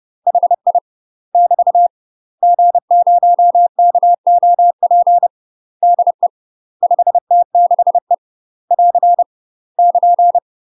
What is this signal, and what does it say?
Morse code, heard as a keyed tone, about 30 wpm, 700 Hz. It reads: HI <BT> G0KOP DE 5T6E <AR> <KN>
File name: UK MASTER QSOs TRAINING / 30WPM / UK MASTER QSOs 0024